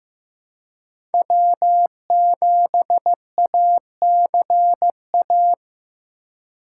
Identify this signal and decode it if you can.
Morse code, heard as a keyed tone, about 15 wpm, 700 Hz. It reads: W7ACA
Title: USA Call Signs - 2x Character Spacing - 15wpm